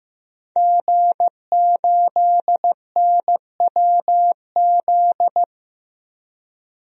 Morse code, heard as a keyed tone, about 15 wpm, 700 Hz. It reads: G8NWZ
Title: DXCC-100 Call Signs - 5 Characters - 15wpm